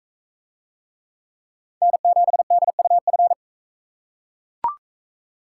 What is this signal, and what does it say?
Morse code, heard as a keyed tone, about 40 wpm, 700 Hz. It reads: N7DUF